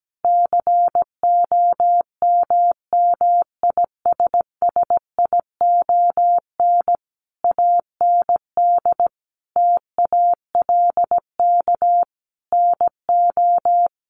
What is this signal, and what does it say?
Morse code, heard as a keyed tone, about 17 wpm, 700 Hz. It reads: COMMISSION AND TALK NO
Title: Sentences from Top 1400 Words 17wpm